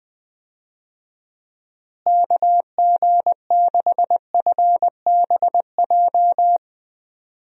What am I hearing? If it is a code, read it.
Morse code, heard as a keyed tone, about 20 wpm, 700 Hz. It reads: KG6FBJ
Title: USA Call Signs - 6 Characters - Repeated 2x - 20wpm